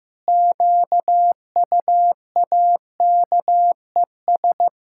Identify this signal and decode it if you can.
Morse code, heard as a keyed tone, about 14 wpm, 700 Hz. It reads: QUAKES
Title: Pangram Challenge v2 - 15wpm